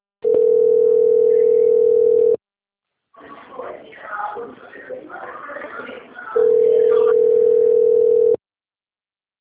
Ringback tone (North America), 440 + 480 Hz.